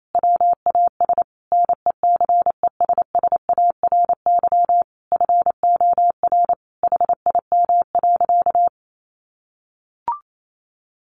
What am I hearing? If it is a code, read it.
Morse code, heard as a keyed tone, about 28 wpm, 700 Hz. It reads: WAS NECESSARY FOR HIM.
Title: Sentences from Top 1500 Words 28wpm